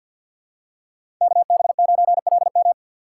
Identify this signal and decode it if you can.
Morse code, heard as a keyed tone, about 53 wpm, 700 Hz. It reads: KB9LM
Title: USA Call Signs - 5 Characters - 3x Character Spacing - 50wpm